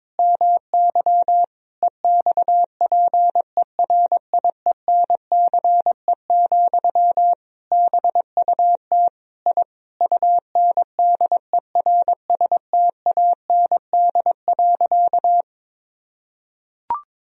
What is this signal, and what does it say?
Morse code, heard as a keyed tone, about 22 wpm, 700 Hz. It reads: MY EXPERIENCE, BUT I UNDERSTAND.
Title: Sentences from Top 1000 Words - Review Part 4 - 22wpm